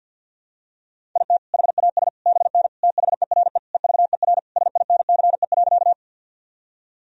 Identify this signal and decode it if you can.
Morse code, heard as a keyed tone, about 53 wpm, 700 Hz. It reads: IT HAS BN THERE EVER SINCE.
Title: Sentences from Top 400 Words and Abbreviations 50wpm